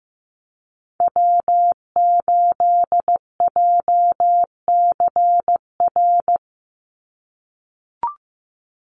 Morse code, heard as a keyed tone, about 15 wpm, 700 Hz. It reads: W8JCR
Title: USA Call Signs 15wpm